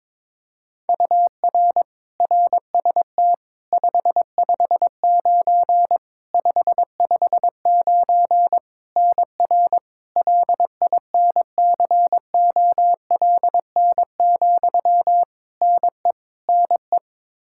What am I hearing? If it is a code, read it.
Morse code, heard as a keyed tone, about 22 wpm, 700 Hz. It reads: UR RST 559 559 NR LINCOLN, NE NE
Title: Practice QSO - Exchange 1 - Beginning - 22wpm